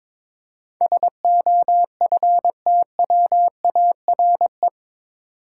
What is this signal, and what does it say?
Morse code, heard as a keyed tone, about 21 wpm, 700 Hz. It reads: SOFTWARE
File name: practice / 22wpm / Eight Letter Words - 22wpm